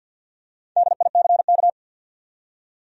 Morse code, heard as a keyed tone, about 53 wpm, 700 Hz. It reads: DICK
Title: USA Common Names - 50wpm